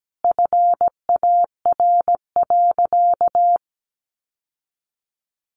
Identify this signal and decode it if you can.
Morse code, heard as a keyed tone, about 17 wpm, 700 Hz. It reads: FAR.
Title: Sentences from Top 600 Words 17wpm